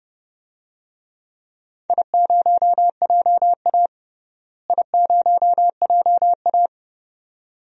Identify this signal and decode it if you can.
Morse code, heard as a keyed tone, about 30 wpm, 700 Hz. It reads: I0JA I0JA
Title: DXCC-100 Call Signs - 4 Characters - Repeated 2x - 30wpm